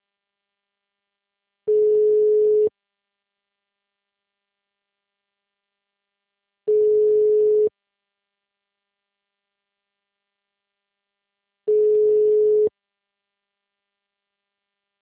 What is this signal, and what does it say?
Ringback tone (Europe, 425 Hz).